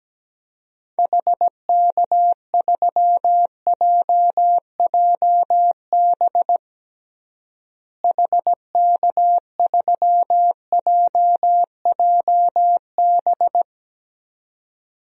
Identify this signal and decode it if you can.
Morse code, heard as a keyed tone, about 17 wpm, 700 Hz. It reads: HK3JJB HK3JJB
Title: DXCC-100 Call Signs - 6 Characters - Repeated 2x - 17wpm